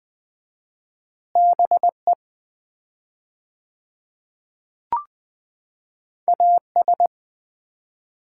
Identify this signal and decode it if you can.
Morse code, heard as a keyed tone, about 20 wpm, 700 Hz. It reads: BE AS
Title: 4x Word-Spacing - Sets of 3 Words - 20wpm